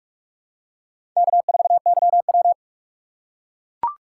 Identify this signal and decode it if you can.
Morse code, heard as a keyed tone, about 44 wpm, 700 Hz. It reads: K4YW
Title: USA Call Signs - 4 Characters - 6x Character Spacing - 45wpm